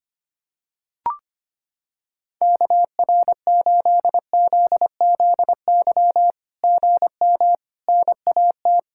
Morse code, heard as a keyed tone, about 25 wpm, 700 Hz. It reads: KR8ZZY GM NAT
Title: SOTA QSO Practice 25wpm